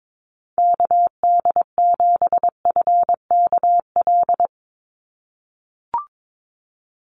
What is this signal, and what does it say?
Morse code, heard as a keyed tone, about 22 wpm, 700 Hz. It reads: KD7FKL
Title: USA Call Signs 22wpm